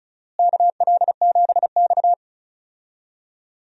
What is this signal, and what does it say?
Morse code, heard as a keyed tone, about 34 wpm, 700 Hz. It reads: KL7X
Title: DXCC-100 Call Signs - 4 Characters - Repeated 2x - 35wpm